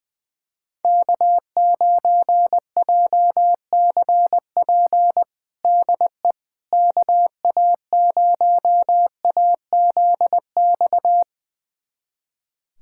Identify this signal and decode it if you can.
Morse code, heard as a keyed tone, about 20 wpm, 700 Hz. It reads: K9JCP DE KA0AZX
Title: USA Callsign DE Callsign 20wpm